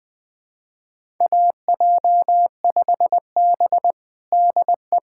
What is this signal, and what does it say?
Morse code, heard as a keyed tone, about 20 wpm, 700 Hz. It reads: AJ5B DE
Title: USA Callsign DE Callsign - 4 Characters - 20wpm